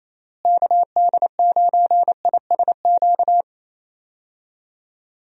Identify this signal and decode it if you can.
Morse code, heard as a keyed tone, about 28 wpm, 700 Hz. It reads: KD9ISQ
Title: USA Call Signs - 6 Characters - Repeated 3x - 28wpm